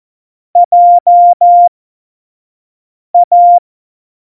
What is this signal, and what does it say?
Morse code, heard as a keyed tone, about 14 wpm, spaced as for 5 wpm, 700 Hz. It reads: JA
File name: Morsekurs / Witze / Witz 24.Dezember 2019, 25 F70